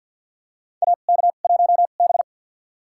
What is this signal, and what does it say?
Morse code, heard as a keyed tone, about 52 wpm, spaced as for 37 wpm, 700 Hz. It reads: AK1B
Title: USA Call Signs - 4 Characters - 2x Character Spacing - 50wpm